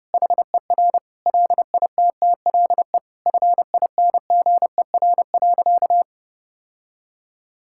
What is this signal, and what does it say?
Morse code, heard as a keyed tone, about 28 wpm, 700 Hz. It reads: HER LITTLE FINGER.
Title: Sentences from Top 1600 Words 30wpm